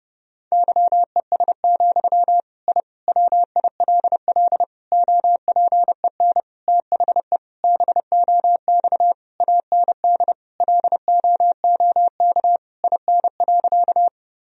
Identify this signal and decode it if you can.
Morse code, heard as a keyed tone, about 30 wpm, 700 Hz. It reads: YES, I WILL OPEN THE BOX AND LOOK IN.
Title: Sentences from Top 1000 Words - Review Part 2 - 30wpm